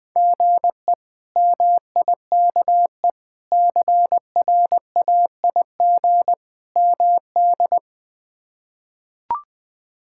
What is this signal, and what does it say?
Morse code, heard as a keyed tone, about 20 wpm, 700 Hz. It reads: GE MIKE CRAIG MD
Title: K1USN SST QSO Practice - 20wpm